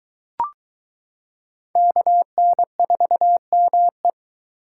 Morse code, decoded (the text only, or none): KN4ME